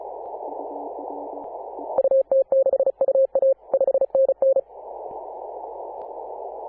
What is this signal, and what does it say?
Morse code, heard as a keyed tone, about 34 wpm, 540 Hz. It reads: UT6UA 5NN
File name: Apr/29 2042z JD1BLY 14.011MHz CW